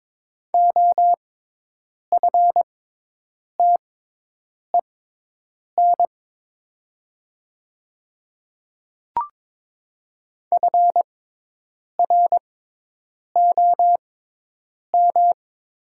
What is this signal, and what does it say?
Morse code, heard as a keyed tone, about 21 wpm, spaced as for 8 wpm, 700 Hz. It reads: OFTEN FROM
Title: 6x Character Spacing - Top 500 Words - 22wpm